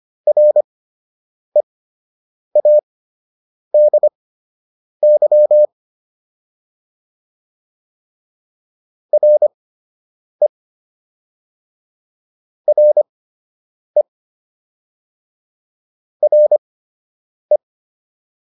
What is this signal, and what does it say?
Morse code, heard as a keyed tone, about 25 wpm, spaced as for 8 wpm, 600 Hz. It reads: READY RE RE RE